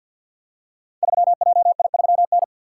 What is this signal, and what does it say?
Morse code, heard as a keyed tone, about 47 wpm, 700 Hz. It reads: 3JI3N
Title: Code Groups - 5 Letter:Numbers - 50wpm